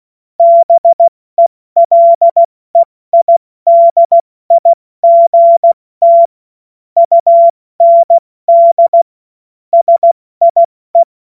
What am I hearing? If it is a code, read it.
Morse code, heard as a keyed tone, about 16 wpm, spaced as for 14 wpm, 680 Hz. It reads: BELEIDIGT UND SIE
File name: Morsekurs / Witze / Witz 30. Dezember 2025 70 F80